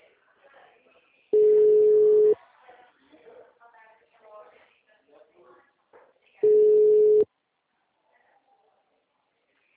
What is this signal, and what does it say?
Ringback tone (Europe, 425 Hz), with speech.
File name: out- -2017- - - .101